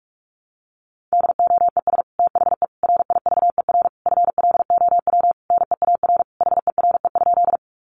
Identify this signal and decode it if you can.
Morse code, heard as a keyed tone, about 44 wpm, 700 Hz. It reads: DOES THE RIVER FLOW NEAR HERE?
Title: Sentences from Top 1200 Words 45wpm